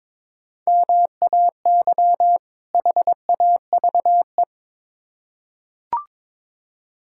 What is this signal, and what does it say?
Morse code, heard as a keyed tone, about 22 wpm, 700 Hz. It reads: MAY HAVE
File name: Build A Phrase - 2-Word Phrases - 22wpm